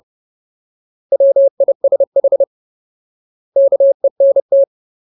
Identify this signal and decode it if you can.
Morse code, heard as a keyed tone, about 30 wpm, 550 Hz. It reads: WISH KENT
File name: Ultimate Word List / 30WPM Ultimate